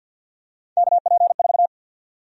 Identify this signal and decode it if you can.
Morse code, heard as a keyed tone, about 53 wpm, 700 Hz. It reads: KP4
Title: USA Call Sign Prefixes - 50wpm